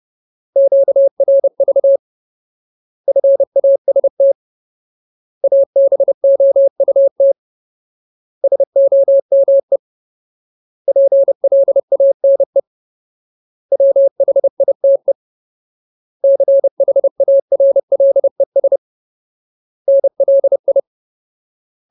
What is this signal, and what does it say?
Morse code, heard as a keyed tone, about 30 wpm, 550 Hz. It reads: QRV FAST ABOUT SOME PLANE WHITE CHARLES NLI
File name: Ultimate Word List / 30WPM Ultimate